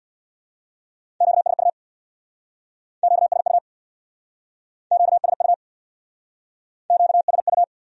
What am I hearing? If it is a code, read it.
Morse code, heard as a keyed tone, about 61 wpm, 700 Hz. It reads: QSU QSU QSU QSU
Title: Speed-Racing Q-Codes - 50wpm